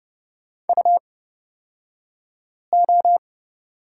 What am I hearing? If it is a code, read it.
Morse code, heard as a keyed tone, about 30 wpm, spaced as for 5 wpm, 700 Hz. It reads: UO
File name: Lesson 36d - BT - Two Characters 30wpm